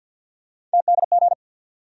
Morse code, heard as a keyed tone, about 50 wpm, 700 Hz. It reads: TDG
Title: USA Call Sign Postfixes - 3 Letters 50wpm